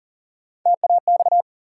Morse code, heard as a keyed tone, about 40 wpm, 700 Hz. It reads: TAX
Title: Three Letter Words - 40wpm